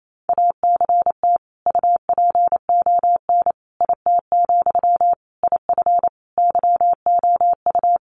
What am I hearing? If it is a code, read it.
Morse code, heard as a keyed tone, about 28 wpm, 700 Hz. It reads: ACT UPON IT, IF YOU
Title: Sentences from Top 1100 Words 28wpm